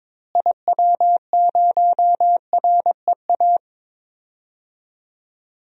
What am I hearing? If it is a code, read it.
Morse code, heard as a keyed tone, about 21 wpm, 700 Hz. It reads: IW0REA
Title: DXCC-100 Call Signs - 6 Characters - Repeated 2x - 22wpm